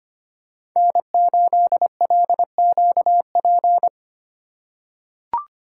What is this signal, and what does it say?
Morse code, heard as a keyed tone, about 25 wpm, 700 Hz. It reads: N8LQP
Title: USA Call Signs - 5 Characters - 4x Character Spacing - 25wpm